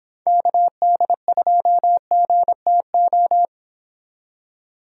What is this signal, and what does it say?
Morse code, heard as a keyed tone, about 25 wpm, 700 Hz. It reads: KD2GTO